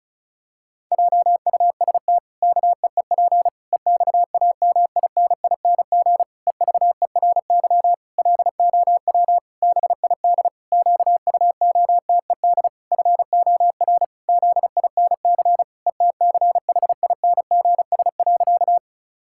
Morse code, heard as a keyed tone, about 34 wpm, 700 Hz. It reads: JUST KEEP EXAMINING EVERY LOW BID QUOTED FOR ZINC ETCHINGS.
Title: Pangram Challenge v2 - 35wpm